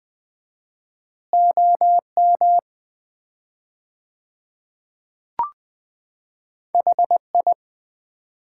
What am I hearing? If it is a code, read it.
Morse code, heard as a keyed tone, about 20 wpm, 700 Hz. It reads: OM HI
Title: Common QSO Phrases - 4x Word-Spacing - 20wpm